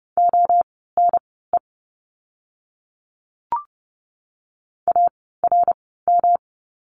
Morse code, heard as a keyed tone, about 30 wpm, spaced as for 17 wpm, 700 Hz. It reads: ONE ARM
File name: course / 30:10wpm / 108 Lesson 18f - Letter m - Words 30wpm